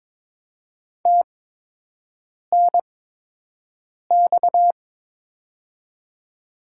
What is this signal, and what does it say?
Morse code, heard as a keyed tone, about 21 wpm, spaced as for 6 wpm, 700 Hz. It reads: TNX
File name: Top 90 QSO Elements - 8x Character Spacing - 22wpm